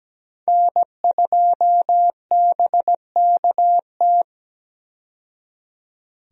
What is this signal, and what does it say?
Morse code, heard as a keyed tone, about 17 wpm, 700 Hz. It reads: N2BKT
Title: USA Call Signs - 5 Characters - 17wpm